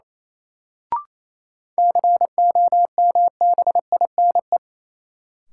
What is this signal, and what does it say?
Morse code, heard as a keyed tone, about 28 wpm, 700 Hz. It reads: COMBINE
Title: Sentences from Top 1600 Words - New Word Drill - 28wpm